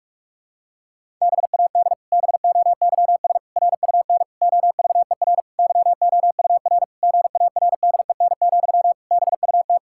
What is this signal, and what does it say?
Morse code, heard as a keyed tone, about 46 wpm, 700 Hz. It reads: BAD BOYS RUN OVER YOUR GARDEN, BUT
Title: Pangram Challenge v2 - 45wpm